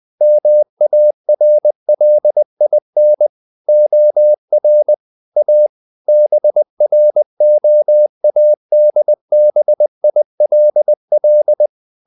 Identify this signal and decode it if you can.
Morse code, heard as a keyed tone, about 20 wpm, 600 Hz. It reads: MARLIN OR A BROADBILL
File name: the old man and the sea / the old man and the sea part 1 20wpm